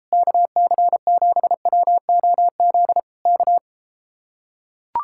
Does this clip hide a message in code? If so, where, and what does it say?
Yes: Morse code from 0.1 to 3.6 s, heard as a keyed tone, about 33 wpm, 700 Hz. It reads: KC7WOZ K